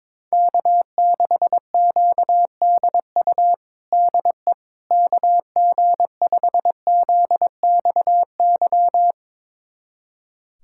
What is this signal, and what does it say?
Morse code, heard as a keyed tone, about 22 wpm, 700 Hz. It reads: K6QDU DE KG5ZXY
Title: USA Callsign DE Callsign 22wpm